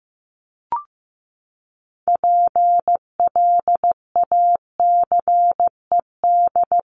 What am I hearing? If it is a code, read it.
Morse code, heard as a keyed tone, about 15 wpm, 700 Hz. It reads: PLACED
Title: ED Suffix - Top 100 Words with Base Words 15wpm